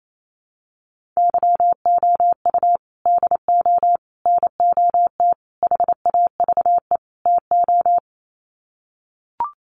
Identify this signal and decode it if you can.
Morse code, heard as a keyed tone, about 28 wpm, 700 Hz. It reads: YOU DO NOT HAVE TO
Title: Sets of 5 Words 28wpm